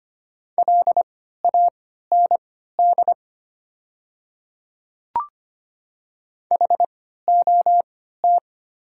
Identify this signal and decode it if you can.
Morse code, heard as a keyed tone, about 25 wpm, spaced as for 14 wpm, 690 Hz. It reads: LAND HOT
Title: 3x Character Spacing - Top 200 Words - 25wpm